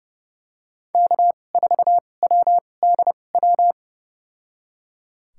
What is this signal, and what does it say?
Morse code, heard as a keyed tone, about 30 wpm, spaced as for 22 wpm, 700 Hz. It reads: K4WDW
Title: USA Call Signs - 5 Characters - 2x Character Spacing - 30wpm